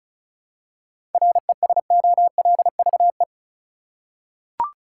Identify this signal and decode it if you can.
Morse code, heard as a keyed tone, about 36 wpm, 700 Hz. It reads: RESOLVE
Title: Seven Letter Words - 35wpm